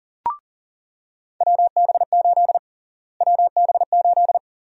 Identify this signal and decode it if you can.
Morse code, heard as a keyed tone, about 40 wpm, 700 Hz. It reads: WB8 WB8
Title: USA Call Sign Prefixes - Repeated 2x - 40wpm